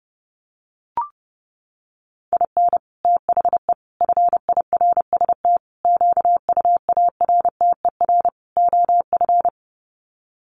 Morse code, heard as a keyed tone, about 30 wpm, 700 Hz. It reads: IN THE FIRST QUARTER OF
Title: Sets of 5 Words Encore2 30wpm